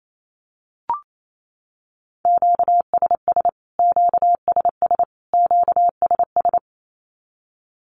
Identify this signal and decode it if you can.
Morse code, heard as a keyed tone, about 28 wpm, 700 Hz. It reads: QSS QSS QSS
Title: Q-Codes Repeated 3x 28wpm